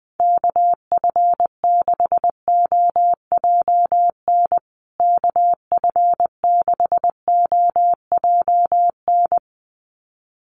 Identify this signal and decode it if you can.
Morse code, heard as a keyed tone, about 20 wpm, 700 Hz. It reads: KF6OJN KF6OJN
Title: USA Call Signs Repeated 2x 20wpm